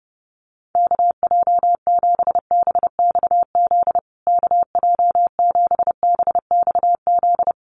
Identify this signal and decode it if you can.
Morse code, heard as a keyed tone, about 28 wpm, 700 Hz. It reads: KJ7BXZ KJ7BXZ
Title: USA Call Signs Repeated 3x 30wpm